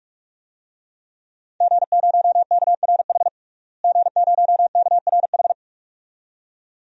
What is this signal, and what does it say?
Morse code, heard as a keyed tone, about 46 wpm, 700 Hz. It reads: G0KRH G0KRH
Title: DXCC-100 Call Signs - 5 Characters - Repeated 2x - 45wpm